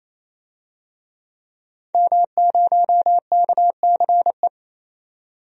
Morse code, heard as a keyed tone, about 28 wpm, 700 Hz. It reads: M0KCE
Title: DXCC-100 Call Signs - 5 Characters - 28wpm